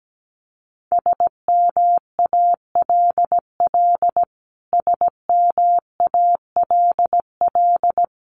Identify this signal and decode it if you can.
Morse code, heard as a keyed tone, about 17 wpm, 700 Hz. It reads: SMALL SMALL
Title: Top 200 Words Repeated 17wpm